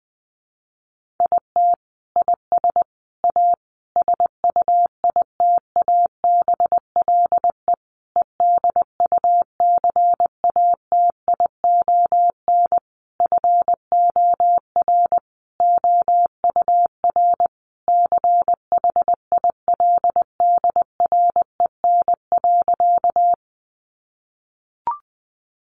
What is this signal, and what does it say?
Morse code, heard as a keyed tone, about 20 wpm, 700 Hz. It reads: IT IS A SUITABLE EDUCATION FOR OUR CHILDREN.
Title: Sentences from Top 1400 Words 20wpm